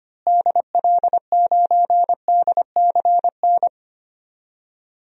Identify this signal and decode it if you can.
Morse code, heard as a keyed tone, about 25 wpm, 700 Hz. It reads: DL9DCN
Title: DXCC-100 Call Signs - 6 Characters - Repeated 3x - 25wpm